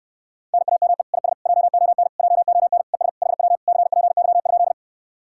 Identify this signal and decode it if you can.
Morse code, heard as a keyed tone, about 69 wpm, 700 Hz. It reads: NAME IS JON JON ES HW CPY?